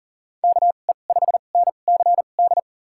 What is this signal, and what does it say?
Morse code, heard as a keyed tone, about 40 wpm, spaced as for 29 wpm, 700 Hz. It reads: KE5NCD
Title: USA Call Signs - 6 Characters - 2x Character Spacing - 40wpm